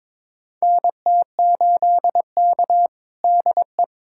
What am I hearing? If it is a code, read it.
Morse code, heard as a keyed tone, about 22 wpm, 700 Hz. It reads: NT8K DE
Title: USA Callsign DE Callsign - 4 Characters - 22wpm